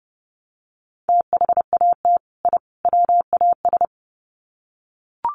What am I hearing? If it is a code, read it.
Morse code, heard as a keyed tone, about 30 wpm, 700 Hz. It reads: THAT I WAS
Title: Sets of 3 Words 30wpm